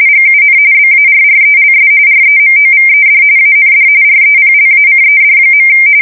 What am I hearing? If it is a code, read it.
RTTY, about 45.45 baud. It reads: GUN BARRELS REMOVED AND REPLACED.
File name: fsk170 / USS-New-Jersey